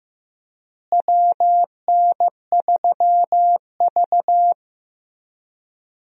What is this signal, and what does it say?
Morse code, heard as a keyed tone, about 15 wpm, 700 Hz. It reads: WN3V
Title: USA Call Signs - 4 Characters - Repeated 3x - 15wpm